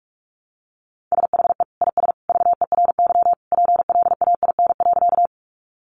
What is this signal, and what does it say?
Morse code, heard as a keyed tone, about 44 wpm, 700 Hz. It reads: SHE IS VERY PLAIN.
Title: Sentences from Top 400 Words 45wpm